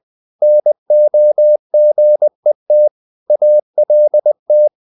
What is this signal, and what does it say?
Morse code, heard as a keyed tone, about 20 wpm, 600 Hz. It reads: NOGET ALT
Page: Ferie 20 wpm